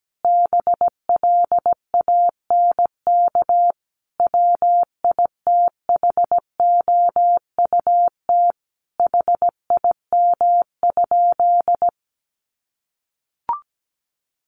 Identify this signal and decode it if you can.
Morse code, heard as a keyed tone, about 17 wpm, 700 Hz. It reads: BLANK WITHOUT HIM?
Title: Sentences from Top 1500 Words 17wpm